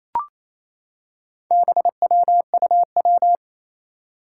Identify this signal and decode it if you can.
Morse code, heard as a keyed tone, about 28 wpm, 700 Hz. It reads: BWUW
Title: Four Challenging Letter-Number Combinations 28wpm